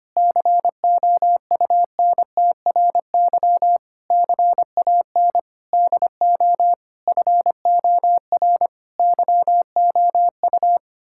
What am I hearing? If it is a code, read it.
Morse code, heard as a keyed tone, about 25 wpm, 700 Hz. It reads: COUNTRY CAN DO FOR YOU
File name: Sets of 9 Words - 25wpm